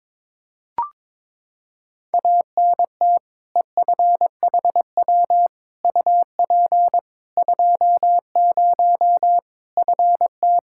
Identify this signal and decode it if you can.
Morse code, heard as a keyed tone, about 22 wpm, 700 Hz. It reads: ANT EFHW UP 20 FT
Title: Practice QSO - Exchange 2 - Info - 22wpm